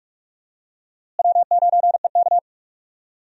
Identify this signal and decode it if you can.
Morse code, heard as a keyed tone, about 46 wpm, 700 Hz. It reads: W9EK
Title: USA Call Signs - 4 Characters - Repeated 2x - 45wpm